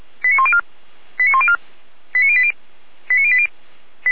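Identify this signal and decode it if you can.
Five-tone selective call (ZVEI), heard as quick runs of five tones, tones of about 70 ms. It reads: 89104 89104 89080 89080
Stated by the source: ZVEI five-tone call received by SDR.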